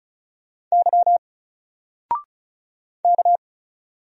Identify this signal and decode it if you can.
Morse code, heard as a keyed tone, about 36 wpm, 700 Hz. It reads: Y K
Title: Single Letter-Number 35wpm